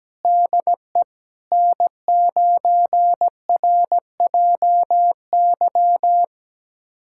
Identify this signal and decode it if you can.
Morse code, heard as a keyed tone, about 17 wpm, 700 Hz. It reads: DE N9RJY
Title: USA Callsign DE Callsign - 5 Characters - 17wpm